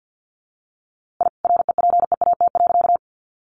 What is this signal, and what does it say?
Morse code, heard as a keyed tone, about 49 wpm, 700 Hz. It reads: I REPEAT.
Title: A Sound Of Thunder 50wpm